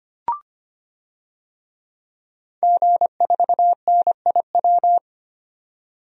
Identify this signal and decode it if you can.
Morse code, heard as a keyed tone, about 25 wpm, 700 Hz. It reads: G4NIW
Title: DXCC-100 Call Signs - 5 Characters - 25wpm